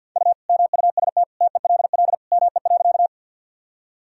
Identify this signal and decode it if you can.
Morse code, heard as a keyed tone, about 48 wpm, 700 Hz. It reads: U MUST TELL ME.
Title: Sentences from Top 200 Words and Abbreviations 50wpm